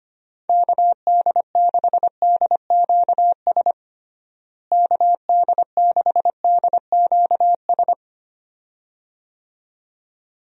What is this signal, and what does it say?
Morse code, heard as a keyed tone, about 25 wpm, 700 Hz. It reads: KD6DQS KD6DQS
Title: USA Call Signs - 6 Characters - Repeated 2x - 25wpm